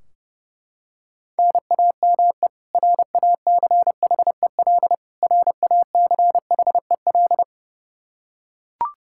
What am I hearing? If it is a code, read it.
Morse code, heard as a keyed tone, about 30 wpm, 700 Hz. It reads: NAME RACHEL RACHEL
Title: Practice QSO - Exchange 1 - Beginning - 30wpm